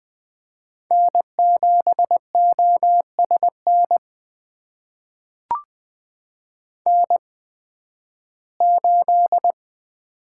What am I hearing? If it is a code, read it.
Morse code, heard as a keyed tone, about 20 wpm, spaced as for 13 wpm, 700 Hz. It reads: N7OSN N8
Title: USA Call Signs - 5 Characters - 8x Character Spacing - 20wpm